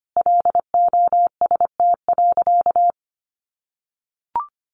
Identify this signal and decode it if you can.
Morse code, heard as a keyed tone, about 25 wpm, 700 Hz. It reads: LOST.
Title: Sentences from Top 1400 Words 25wpm